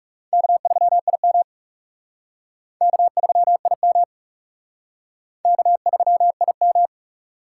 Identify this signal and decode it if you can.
Morse code, heard as a keyed tone, about 36 wpm, 700 Hz. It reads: K3IM K3IM K3IM
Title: Speed Racing - USA 1x2 Call Signs 30wpm